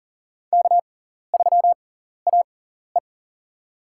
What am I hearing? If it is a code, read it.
Morse code, heard as a keyed tone, about 40 wpm, spaced as for 14 wpm, 700 Hz. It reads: K3AE
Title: USA Call Signs - 4 Characters - 6x Character Spacing - 40wpm